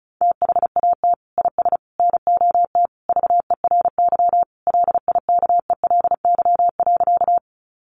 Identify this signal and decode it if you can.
Morse code, heard as a keyed tone, about 34 wpm, 700 Hz. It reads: THAT IS NOT VERY LIKELY.
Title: Sentences from Top 600 Words 35wpm